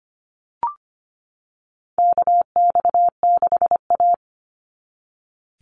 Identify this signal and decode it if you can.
Morse code, heard as a keyed tone, about 25 wpm, 700 Hz. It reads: KX6A
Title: NA SOTA Chaser Callsigns 25wpm